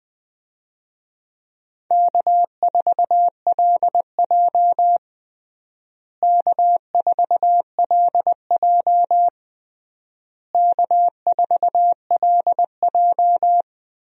Morse code, heard as a keyed tone, about 21 wpm, 700 Hz. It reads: K4LJ K4LJ K4LJ
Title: USA Call Signs - 4 Characters - Repeated 3x - 20wpm